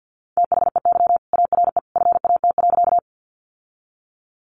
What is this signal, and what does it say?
Morse code, heard as a keyed tone, about 50 wpm, 700 Hz. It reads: THEY ARE FAT.
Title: Sentences from Top 600 Words 50wpm